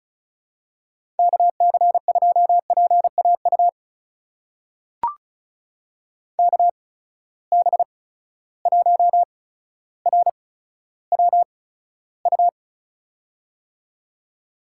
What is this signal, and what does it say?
Morse code, heard as a keyed tone, about 36 wpm, spaced as for 15 wpm, 700 Hz. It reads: KC2PAU KB1RWU
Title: USA Call Signs - 6 Characters - 8x Character Spacing - 35wpm